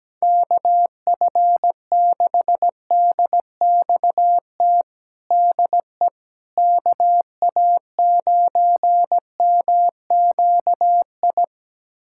Morse code, heard as a keyed tone, about 17 wpm, 700 Hz. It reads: KF6DXT DE KA9MQI
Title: USA Callsign DE Callsign - 6 Characters - 17wpm